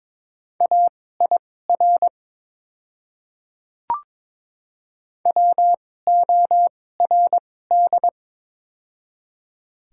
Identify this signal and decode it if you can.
Morse code, heard as a keyed tone, about 22 wpm, spaced as for 16 wpm, 700 Hz. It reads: AIR WORD
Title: 2x Character Spacing - Top 500 Words - 22wpm